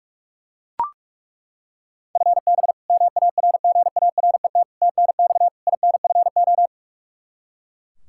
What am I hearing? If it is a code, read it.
Morse code, heard as a keyed tone, about 42 wpm, 700 Hz. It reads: FB MARGARET TNX INFO
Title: Practice QSO - Exchange 3 - More Info - 45wpm